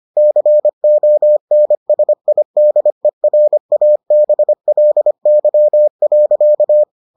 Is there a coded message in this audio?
Yes: Morse code from 0.2 to 6.8 s, heard as a keyed tone, about 25 wpm, 600 Hz. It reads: CONSIDERABLY.